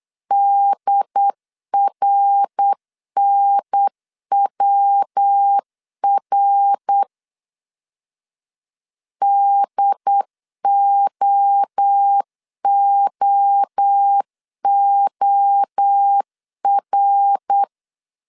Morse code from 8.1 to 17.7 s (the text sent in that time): DOOOR